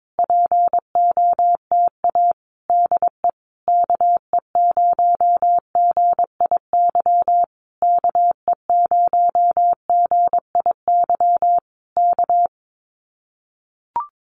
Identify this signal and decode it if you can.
Morse code, heard as a keyed tone, about 22 wpm, 700 Hz. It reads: POTA DE KE0GIY KE0GIY K
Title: Common QSO Phrases Encore 22wpm